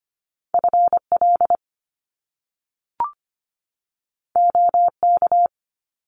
Morse code, heard as a keyed tone, about 25 wpm, 700 Hz. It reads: FL OK
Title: US States 25wpm